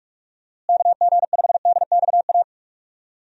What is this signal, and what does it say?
Morse code, heard as a keyed tone, about 45 wpm, 700 Hz. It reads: KG5DXA
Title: USA Call Signs - 6 Characters - Repeated 2x - 45wpm